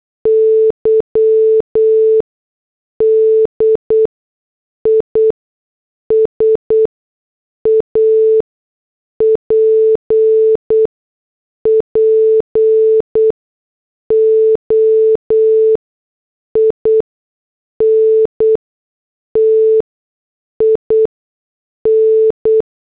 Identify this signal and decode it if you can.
Morse code, heard as a keyed tone, about 8 wpm, spaced as for 6 wpm, 440 Hz. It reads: Y DISAPPOINTIN
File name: morse / headlines